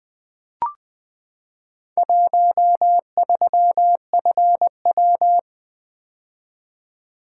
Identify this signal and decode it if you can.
Morse code, heard as a keyed tone, about 21 wpm, 700 Hz. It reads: 13FW
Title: Four Letter-Number Combinations - 20wpm